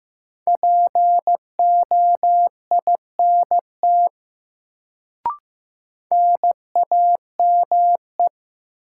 Morse code, heard as a keyed tone, about 15 wpm, 700 Hz. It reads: POINT NAME
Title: Sentences from Top 200 Words - New Word Drill - 15wpm